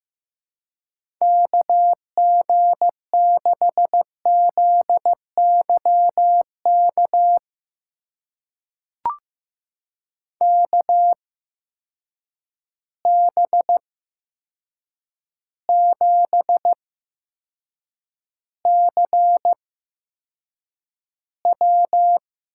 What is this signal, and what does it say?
Morse code, heard as a keyed tone, about 15 wpm, spaced as for 7 wpm, 700 Hz. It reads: KG6ZYK KB7CW
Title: USA Call Signs - 6 Characters - 8x Character Spacing - 15wpm